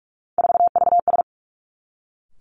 Morse code, heard as a keyed tone, about 46 wpm, 700 Hz. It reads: 4VS
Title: Three Challenging Letter-Number Combinations 45wpm